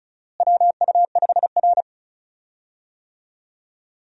Morse code, heard as a keyed tone, about 36 wpm, 700 Hz. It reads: WU5R
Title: USA Call Signs - 4 Characters - Repeated 3x - 35wpm